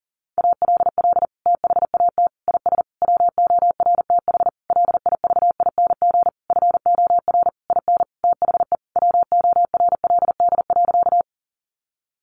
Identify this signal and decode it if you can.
Morse code, heard as a keyed tone, about 40 wpm, 700 Hz. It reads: ALL THAT IS WORTH LIVING FOR IN THE WORLD.
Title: Sentences from Top 1100 Words 40wpm